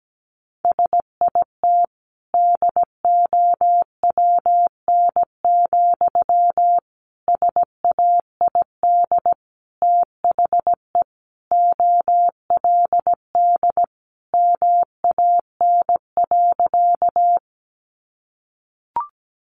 Morse code, heard as a keyed tone, about 17 wpm, 700 Hz. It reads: SIT DOWN, SAID THE OLD MAN.
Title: Sentences from Top 500 Words 17wpm